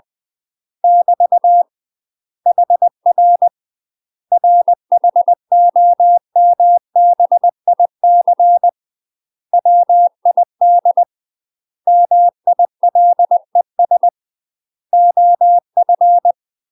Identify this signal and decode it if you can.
Morse code, heard as a keyed tone, about 20 wpm, 700 Hz. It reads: <BT> HR RHOMBIC WID MILES OF
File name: UK MASTER QSOs TRAINING / 20WPM / UK MASTER QSOs 0031